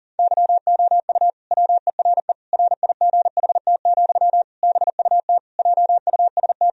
Morse code, heard as a keyed tone, about 37 wpm, 700 Hz. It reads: YOU WERE RIGHT, BUT JUST
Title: Sentences from Top 1000 Words - Review Part 2 - 40wpm